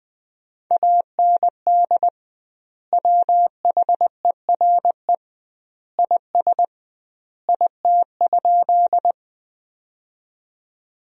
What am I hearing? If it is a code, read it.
Morse code, heard as a keyed tone, about 21 wpm, 700 Hz. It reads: AND WHERE IS IT?
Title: 2x Word-Spacing - Sentences from Top 200 Words - 20wpm